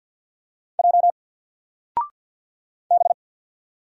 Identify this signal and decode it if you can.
Morse code, heard as a keyed tone, about 50 wpm, 700 Hz. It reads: J B
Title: Single Letter-Number 50wpm